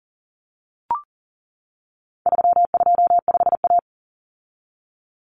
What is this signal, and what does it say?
Morse code, heard as a keyed tone, about 40 wpm, 700 Hz. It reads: 325A
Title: Common Cut Numbers - 4-Digit 40wpm